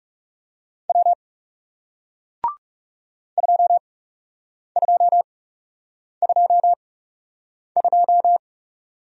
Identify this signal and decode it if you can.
Morse code, heard as a keyed tone, about 39 wpm, spaced as for 8 wpm, 700 Hz. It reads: W 2222